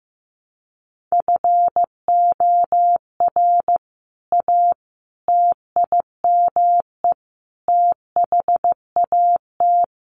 Morse code, heard as a keyed tone, about 15 wpm, 700 Hz. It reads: FOR A TIME THAT
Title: Sentences from Top 700 Words 15wpm